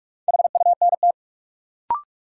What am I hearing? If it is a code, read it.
Morse code, heard as a keyed tone, about 45 wpm, 700 Hz. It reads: HUNT
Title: Sentences from Top 1600 Words - New Word Drill - 45wpm